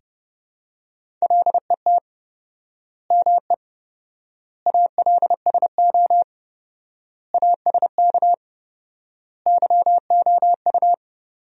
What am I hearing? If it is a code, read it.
Morse code, heard as a keyed tone, about 30 wpm, 700 Hz. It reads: LET ME ALSO ASK YOU
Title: 4x Word-Spacing - Sentences from Top 400 Words - 30wpm